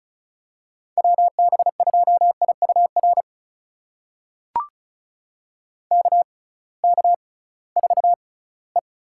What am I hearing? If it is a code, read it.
Morse code, heard as a keyed tone, about 36 wpm, spaced as for 21 wpm, 690 Hz. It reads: WB2IUR KK4E
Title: USA Call Signs - 6 Characters - 6x Character Spacing - 35wpm